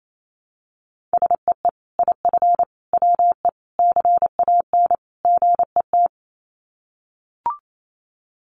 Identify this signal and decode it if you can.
Morse code, heard as a keyed tone, about 28 wpm, 700 Hz. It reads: SEE IF WE CAN GET
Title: Sets of 5 Words Encore2 28wpm